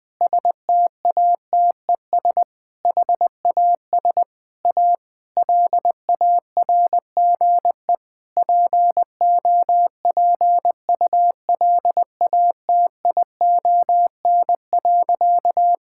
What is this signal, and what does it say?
Morse code, heard as a keyed tone, about 20 wpm, 700 Hz. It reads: STATES HAS A LARGE POPULATION.
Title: Sentences with 4-Word Phrases - 20wpm